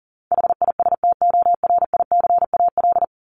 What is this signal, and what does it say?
Morse code, heard as a keyed tone, about 40 wpm, 700 Hz. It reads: HISTORICAL
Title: Top 501-750 Words 40wpm